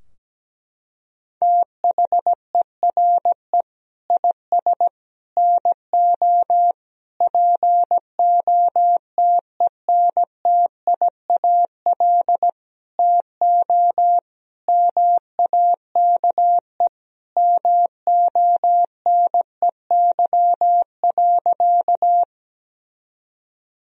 Morse code, heard as a keyed tone, about 17 wpm, 700 Hz. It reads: THERE IS NO POTENTIAL TO MAKE MONEY.
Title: Sentences from Top 1000 Words - Review Part 4 - 17wpm